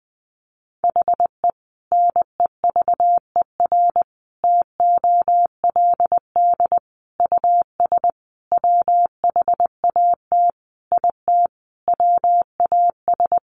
Morse code, heard as a keyed tone, about 20 wpm, 700 Hz. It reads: HE NEVER TOLD US WHAT IT WAS
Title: Sentences from Top 400 Words 20wpm